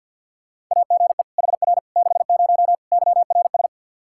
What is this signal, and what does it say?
Morse code, heard as a keyed tone, about 51 wpm, 700 Hz. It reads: AGE HR 60 YRS
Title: Practice QSO - Exchange 3 - More Info - 50wpm